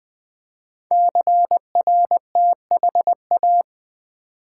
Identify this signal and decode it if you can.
Morse code, heard as a keyed tone, about 20 wpm, 700 Hz. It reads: CRTHA
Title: Code Groups - 5 Letters - 20wpm